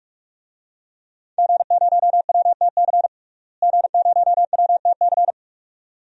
Morse code, heard as a keyed tone, about 46 wpm, 700 Hz. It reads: G0WTC G0WTC
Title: DXCC-100 Call Signs - 5 Characters - Repeated 2x - 45wpm